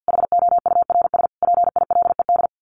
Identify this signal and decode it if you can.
Morse code, heard as a keyed tone, about 50 wpm, 700 Hz. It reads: HOURS PILED
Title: A Sound Of Thunder 50wpm